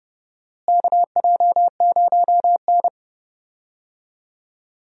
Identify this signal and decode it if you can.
Morse code, heard as a keyed tone, about 30 wpm, 700 Hz. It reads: KJ0N
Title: USA Call Signs - 4 Characters - Repeated 3x - 30wpm